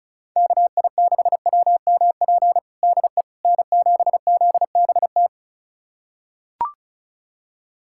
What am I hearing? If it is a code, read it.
Morse code, heard as a keyed tone, about 34 wpm, 700 Hz. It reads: KI6WMP DE N7ZBT
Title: USA Callsign DE Callsign 35wpm